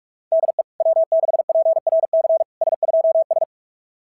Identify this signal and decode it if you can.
Morse code, heard as a keyed tone, about 44 wpm, 660 Hz. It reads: DE W6PRC S2S
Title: SOTA QSO Practice 45wpm